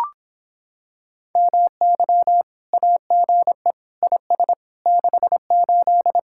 Morse code, heard as a keyed tone, about 26 wpm, 700 Hz. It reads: MY AGE IS 68